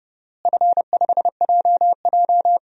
Morse code, heard as a keyed tone, about 32 wpm, 700 Hz. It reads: F5JJ
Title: DXCC-100 Call Signs - 4 Characters - Repeated 3x - 30wpm